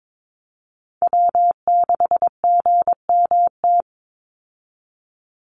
Morse code, heard as a keyed tone, about 22 wpm, 700 Hz. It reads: W6GMT
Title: Call Signs - 22wpm